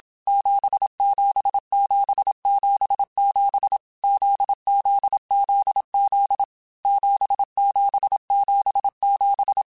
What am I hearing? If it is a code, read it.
Morse code, heard as a keyed tone, about 27 wpm, 800 Hz. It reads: 77777 ZZZZ 7777